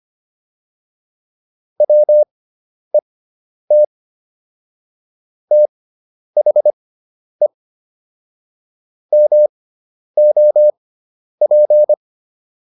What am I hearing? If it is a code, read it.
Morse code, heard as a keyed tone, about 25 wpm, spaced as for 10 wpm, 600 Hz. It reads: WET THE MOP